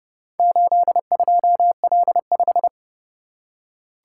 Morse code, heard as a keyed tone, about 29 wpm, 700 Hz. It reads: 82L5